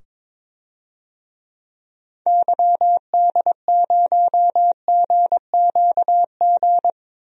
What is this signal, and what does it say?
Morse code, heard as a keyed tone, about 21 wpm, 700 Hz. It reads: YD0GQG
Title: DXCC-100 Call Signs - 6 Characters - 22wpm